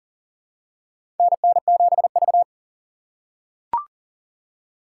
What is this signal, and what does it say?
Morse code, heard as a keyed tone, about 37 wpm, 700 Hz. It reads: NN7V
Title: USA Call Signs - 6x Character Spacing - 40wpm